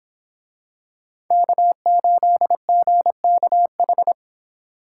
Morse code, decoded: K8GKH

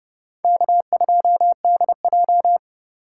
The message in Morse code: K2DJ

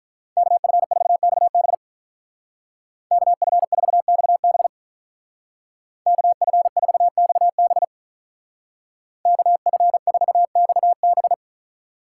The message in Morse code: KF4XB KF4XB KF4XB KF4XB

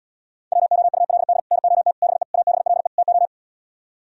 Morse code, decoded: WOULD NOT BE ABLE TO

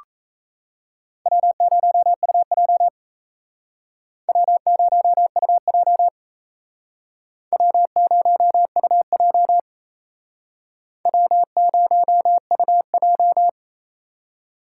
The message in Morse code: W0UJ W0UJ W0UJ W0UJ

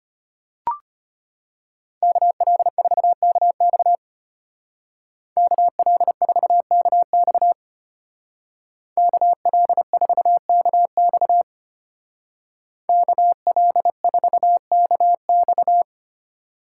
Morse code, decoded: KL4KX KL4KX KL4KX KL4KX

